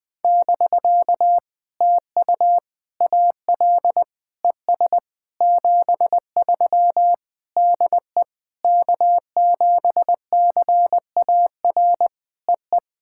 Morse code, heard as a keyed tone, about 21 wpm, 700 Hz. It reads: <BK> TU AL ES 73 DE K7CAR EE
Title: POTA QSO Practice - 20wpm